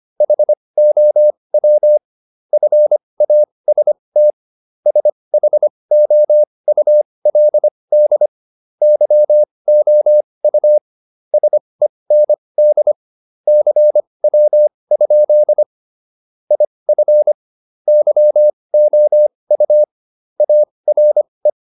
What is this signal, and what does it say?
Morse code, heard as a keyed tone, about 25 wpm, spaced as for 20 wpm, 600 Hz. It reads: HOW FAST SHOULD YOU SEND CW? IF YOU ARE